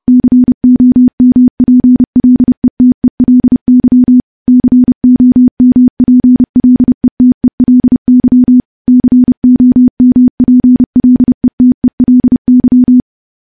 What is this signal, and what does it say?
Morse code, heard as a keyed tone, about 30 wpm, 250 Hz. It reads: COMPLETELY COMPLETELY COMPLETELY
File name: morse / morse-training-30WPM